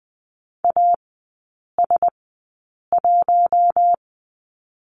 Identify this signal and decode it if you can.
Morse code, heard as a keyed tone, about 20 wpm, spaced as for 8 wpm, 700 Hz. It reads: AS1
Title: 114 Lesson 19e - Letter w - Three Characters 20wpm